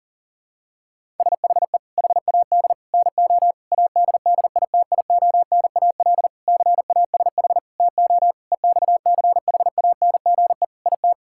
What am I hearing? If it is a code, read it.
Morse code, heard as a keyed tone, about 40 wpm, 700 Hz. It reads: SHE HAD NO ADDITIONAL CASH TO EXCHANGE IT
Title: Sentences from Top 1000 Words - Review Part 6 - 40wpm